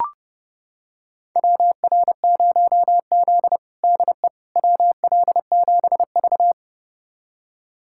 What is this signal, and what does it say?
Morse code, heard as a keyed tone, about 30 wpm, 700 Hz. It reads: WR0Z DE WL7V
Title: USA Callsign DE Callsign - 4 Characters - 30wpm